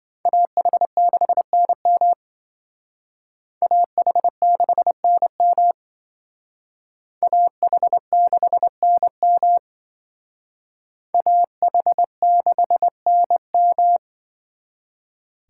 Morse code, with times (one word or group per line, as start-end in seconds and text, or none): 0.3-2.1 AH6NM
3.6-5.7 AH6NM
7.2-9.6 AH6NM
11.1-14.0 AH6NM